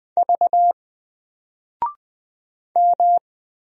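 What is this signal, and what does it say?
Morse code, heard as a keyed tone, about 20 wpm, 700 Hz. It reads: V M
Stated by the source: Single letter and number Morse drill at 20 wpm, from N4LCD.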